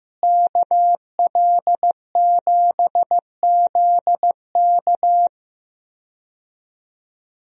Morse code, heard as a keyed tone, about 15 wpm, 700 Hz. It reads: KL7ZK
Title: USA Call Signs - 5 Characters - Repeated 3x - 15wpm